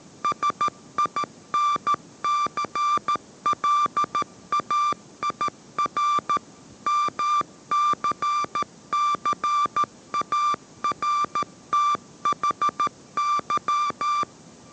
Morse code, heard as a keyed tone, about 13 wpm, 1180 Hz. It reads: SINCLAIR MCCARTHY
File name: 14wpm